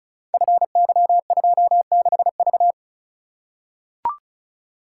Morse code, heard as a keyed tone, about 36 wpm, 700 Hz. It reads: FY26V
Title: Code Groups - 5 Letter:Numbers - 35wpm